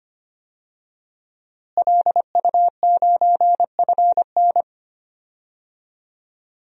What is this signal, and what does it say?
Morse code, heard as a keyed tone, about 25 wpm, 700 Hz. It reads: LU9FN
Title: DXCC-100 Call Signs - 5 Characters - Repeated 3x - 25wpm